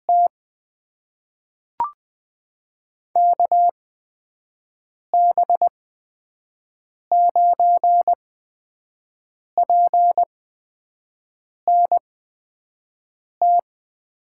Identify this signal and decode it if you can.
Morse code, heard as a keyed tone, about 20 wpm, spaced as for 5 wpm, 700 Hz. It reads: T KB9PNT
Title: USA Call Signs - 6 Characters - 8x Character Spacing - 20wpm